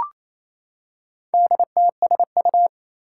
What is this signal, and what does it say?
Morse code, heard as a keyed tone, about 28 wpm, 700 Hz. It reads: DTSU
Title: Four Challenging Letter-Number Combinations 28wpm